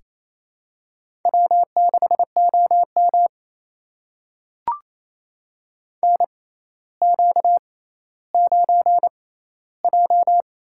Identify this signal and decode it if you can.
Morse code, heard as a keyed tone, about 29 wpm, spaced as for 14 wpm, 700 Hz. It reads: W6OM NQ9J
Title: USA Call Signs - 4 Characters - 6x Character Spacing - 28wpm